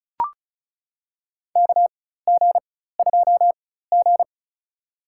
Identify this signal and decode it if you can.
Morse code, heard as a keyed tone, about 36 wpm, spaced as for 16 wpm, 700 Hz. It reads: KG2G
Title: USA Call Signs - 4 Characters - 4x Character Spacing - 35wpm